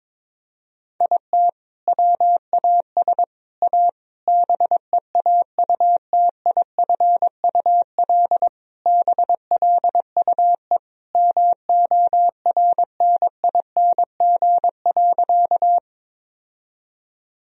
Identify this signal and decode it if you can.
Morse code, heard as a keyed tone, about 23 wpm, 700 Hz. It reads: IT WAS A BEAUTIFUL BLUE MORNING.
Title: Sentences from Top 1000 Words - Review Part 3 - 22wpm